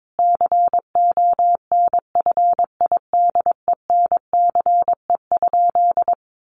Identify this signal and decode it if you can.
Morse code, heard as a keyed tone, about 22 wpm, 700 Hz. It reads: CONFIDENCE?
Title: Sentences from Top 1500 Words 22wpm